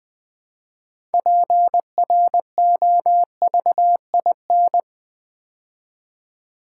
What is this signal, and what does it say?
Morse code, heard as a keyed tone, about 20 wpm, 700 Hz. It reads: PROVIN
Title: Build A Word - 3 Letters At A Time - 8-Letter Words - 20wpm